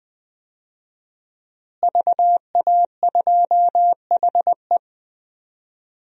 Morse code, heard as a keyed tone, about 20 wpm, 700 Hz. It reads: VA2HE
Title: DXCC-100 Call Signs - 5 Characters - Repeated 2x - 20wpm